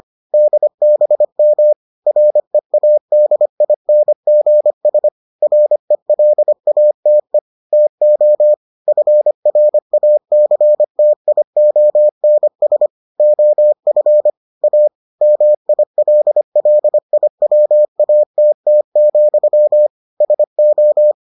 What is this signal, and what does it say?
Morse code, heard as a keyed tone, about 25 wpm, 600 Hz. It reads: DBM READINGS RELATE TO FRACTIONS OF A MILLIWATT, SO